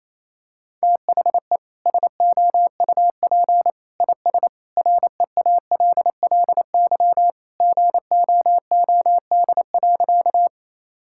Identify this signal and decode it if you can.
Morse code, heard as a keyed tone, about 28 wpm, 700 Hz. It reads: THE SOUP IS REALLY GOOD.
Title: Sentences from Top 1000 Words - Review Part 6 - 28wpm